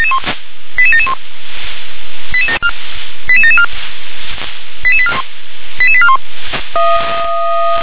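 Five-tone selective call (ZVEI), heard as quick runs of five tones, tones of about 70 ms. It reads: 88771 88774 88841
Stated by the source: ZVEI five-tone selective call received by SDR.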